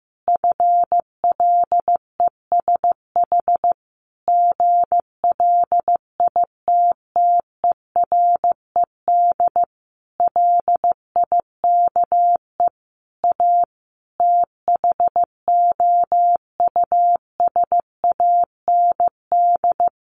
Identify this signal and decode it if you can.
Morse code, heard as a keyed tone, about 15 wpm, 700 Hz. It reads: FLESH GLITTERED LIKE A THOUSAND
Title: A Sound Of Thunder 15wpm